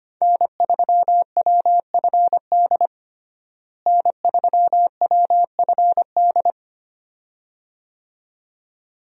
Morse code, heard as a keyed tone, about 25 wpm, 700 Hz. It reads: N3WFD N3WFD
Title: USA Call Signs - 5 Characters - Repeated 2x - 25wpm